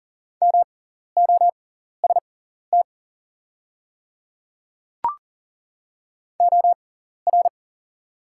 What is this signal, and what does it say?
Morse code, heard as a keyed tone, about 41 wpm, spaced as for 14 wpm, 700 Hz. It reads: MOST OR
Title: 6x Character Spacing - Top 200 Words - 40wpm